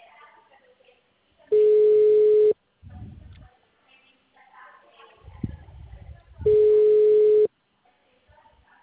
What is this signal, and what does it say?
Ringback tone (Europe, 425 Hz), with speech.
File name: out- -2035